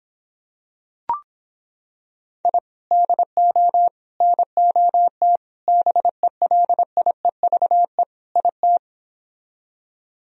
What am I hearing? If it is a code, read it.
Morse code, heard as a keyed tone, about 26 wpm, 700 Hz. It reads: I DO NOT BELIEVE IT